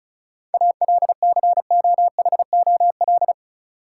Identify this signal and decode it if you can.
Morse code, heard as a keyed tone, about 36 wpm, 700 Hz. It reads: ALCOHOL
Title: Sentences from Top 1400 Words - New Word Drill Alt - 35wpm